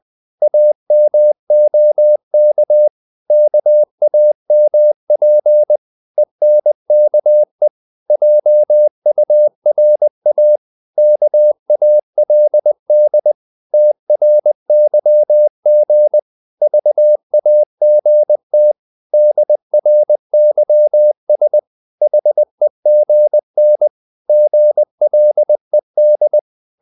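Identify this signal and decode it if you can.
Morse code, heard as a keyed tone, about 20 wpm, 600 Hz. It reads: AMOK KAMP ENKE JURA KALD TRYG VAGT DRYS HEGN GLED
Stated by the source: Morse code practice of short Danish words at 20 wpm, from CW med Gnister.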